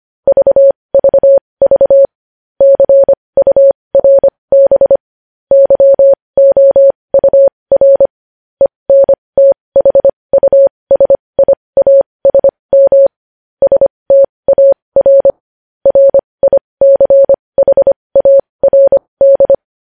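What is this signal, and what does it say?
Morse code, heard as a keyed tone, about 25 wpm, spaced as for 20 wpm, 550 Hz. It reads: VVV CURB YOUR ENTHUSIASM STAR RICHARD
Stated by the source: Morse code news headlines practice at 20 wpm.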